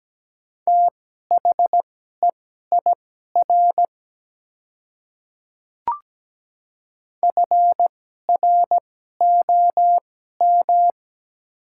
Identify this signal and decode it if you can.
Morse code, heard as a keyed tone, about 17 wpm, spaced as for 12 wpm, 700 Hz. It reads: THEIR FROM
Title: 2x Character Spacing - Top 200 Words - 17wpm